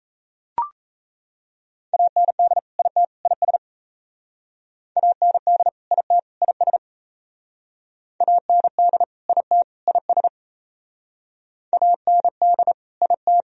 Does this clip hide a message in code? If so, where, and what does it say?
Yes: Morse code from 1.9 to 13.4 s, heard as a keyed tone, about 36 wpm, 700 Hz. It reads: AND IT IS AND IT IS AND IT IS AND IT